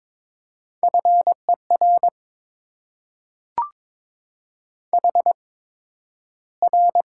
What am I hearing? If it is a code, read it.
Morse code, heard as a keyed tone, about 22 wpm, spaced as for 12 wpm, 700 Hz. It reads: FER HR
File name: Top 90 QSO Elements - 8x Character Spacing - 22wpm